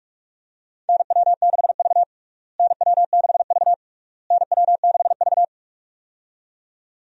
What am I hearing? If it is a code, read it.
Morse code, heard as a keyed tone, about 47 wpm, 700 Hz. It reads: NW6V NW6V NW6V
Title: USA Call Signs - 4 Characters - Repeated 3x - 45wpm